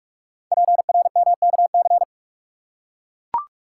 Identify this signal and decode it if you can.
Morse code, heard as a keyed tone, about 44 wpm, 700 Hz. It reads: PRMKC
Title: Code Groups - 5 Letters - 45wpm